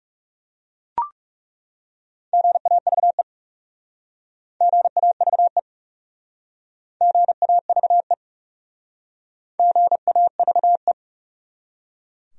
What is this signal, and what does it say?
Morse code, heard as a keyed tone, about 39 wpm, 700 Hz. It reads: GAVE GAVE GAVE GAVE